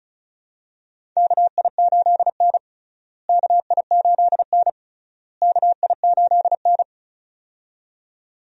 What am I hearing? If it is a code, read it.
Morse code, heard as a keyed tone, about 34 wpm, 700 Hz. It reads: KI8N KI8N KI8N
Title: USA Call Signs - 4 Characters - Repeated 3x - 35wpm